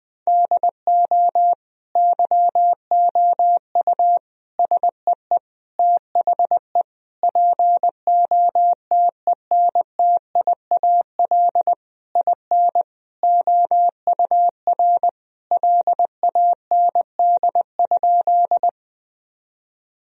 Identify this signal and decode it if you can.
Morse code, heard as a keyed tone, about 20 wpm, 700 Hz. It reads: DO YOU SEE THE POTENTIAL IN OUR LAND?
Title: Sentences from Top 1000 Words - Review Part 4 - 20wpm